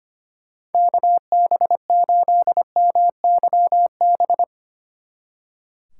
Morse code, heard as a keyed tone, about 32 wpm, 700 Hz. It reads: KB8MYB